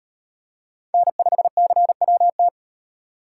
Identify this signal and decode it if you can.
Morse code, heard as a keyed tone, about 38 wpm, 700 Hz. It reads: N5CWT